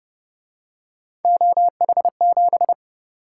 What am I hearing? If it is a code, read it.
Morse code, heard as a keyed tone, about 29 wpm, 700 Hz. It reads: OH7
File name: Speed-Racing - Three Letter-Number Combinations - 25wpm